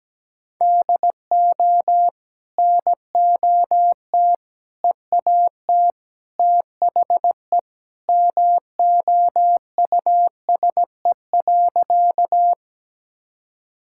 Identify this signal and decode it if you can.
Morse code, heard as a keyed tone, about 17 wpm, 700 Hz. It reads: DO NOT EAT THE MOUSE.
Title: Sentences from Top 1000 Words - Review Part 5 - 17wpm